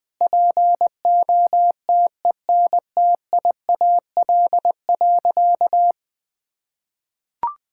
Morse code, heard as a keyed tone, about 20 wpm, 700 Hz. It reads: POTENTIAL.
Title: Sentences with 9-Word Phrases - 20wpm